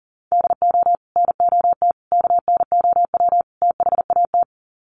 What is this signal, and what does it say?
Morse code, heard as a keyed tone, about 40 wpm, 700 Hz. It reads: DO NOT KNOW THAT
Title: Sets of 4 Words 40wpm